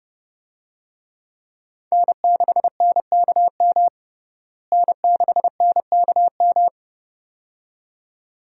Morse code, heard as a keyed tone, about 31 wpm, 700 Hz. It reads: N6NKM N6NKM
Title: USA Call Signs - 5 Characters - Repeated 2x - 30wpm